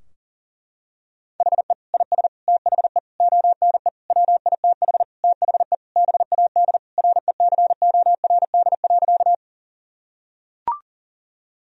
Morse code, heard as a keyed tone, about 37 wpm, 700 Hz. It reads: HE IS THE ONE WITH THE BAD RECORD.
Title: Sentences from Top 1000 Words - Review Part 3 - 40wpm